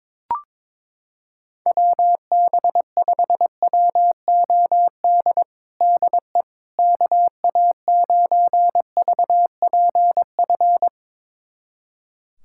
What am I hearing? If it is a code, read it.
Morse code, heard as a keyed tone, about 22 wpm, 700 Hz. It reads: WB5WOD DE KA9VPF
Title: USA Callsign DE Callsign - 6 Characters - 22wpm